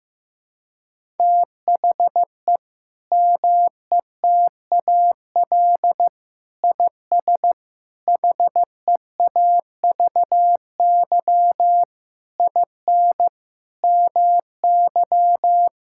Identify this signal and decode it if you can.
Morse code, heard as a keyed tone, about 15 wpm, 700 Hz. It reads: THE METAL IS HEAVY IN MY
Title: Sentences from Top 1000 Words - Review Part 6 - 15wpm